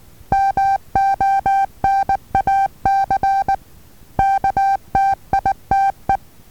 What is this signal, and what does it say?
Morse code, heard as a keyed tone, about 19 wpm, 780 Hz. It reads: MONAC KTITE
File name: Taviro HA6VA B